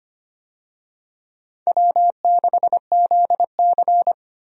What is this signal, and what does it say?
Morse code, heard as a keyed tone, about 25 wpm, 700 Hz. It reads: W6ZC
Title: USA Call Signs - 4 Characters - Repeated 2x - 25wpm